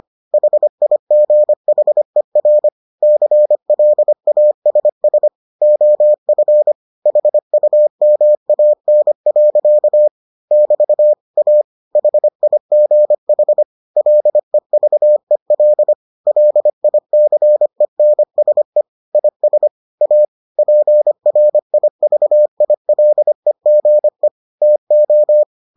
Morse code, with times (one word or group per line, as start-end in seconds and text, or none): 0.3-2.7 HIGHER
3.0-5.3 CLASS
5.6-6.7 OF
7.1-10.1 HUMAN.
10.5-11.1 <BT>
11.4-11.6 A
12.0-13.6 HIGH
14.0-15.9 LEVEL
16.3-18.8 LICENSE
19.1-19.7 IS
20.0-20.2 A
20.6-24.3 PRIVILEGE
24.6-25.4 TO